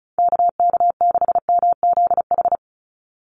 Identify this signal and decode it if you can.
Morse code, heard as a keyed tone, about 34 wpm, 700 Hz. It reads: KK6MZH
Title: USA Call Signs 35wpm